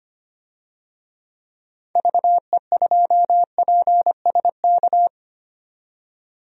Morse code, heard as a keyed tone, about 25 wpm, 700 Hz. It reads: VE2PSK
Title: DXCC-100 Call Signs - 6 Characters - 25wpm